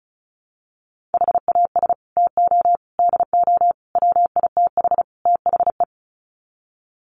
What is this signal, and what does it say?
Morse code, heard as a keyed tone, about 34 wpm, 700 Hz. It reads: HAS TO DO WITH THE
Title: Sets of 5 Words 35wpm